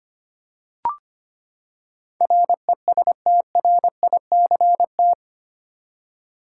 Morse code, heard as a keyed tone, about 25 wpm, 700 Hz. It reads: RESTRICT
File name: Eight Letter Words - 25wpm